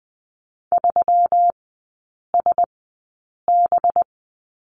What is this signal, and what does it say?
Morse code, heard as a keyed tone, about 20 wpm, spaced as for 8 wpm, 700 Hz. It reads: 3SB
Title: 191 Lesson 31e - Letter v - Three Characters 20wpm